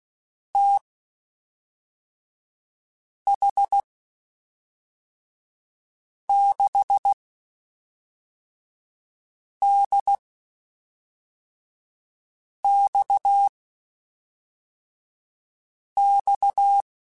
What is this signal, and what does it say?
Morse code, heard as a keyed tone, about 16 wpm, spaced as for 3 wpm, 800 Hz. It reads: TH6DXX
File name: (spaced) / QSOs Spaced-mp3s / WD0BGP 1 @5 sp